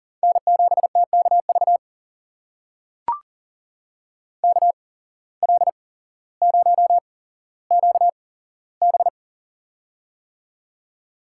Morse code, heard as a keyed tone, about 40 wpm, spaced as for 11 wpm, 700 Hz. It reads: N7TK VKL0QB
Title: USA Call Signs - 5 Characters - 8x Character Spacing - 40wpm